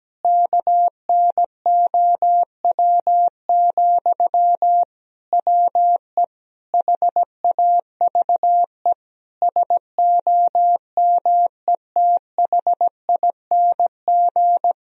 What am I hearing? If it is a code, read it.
Morse code, heard as a keyed tone, about 17 wpm, 700 Hz. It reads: KNOW, WE HAVE SOMETHING
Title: Sentences from Top 1000 Words - Review Part 3 - 17wpm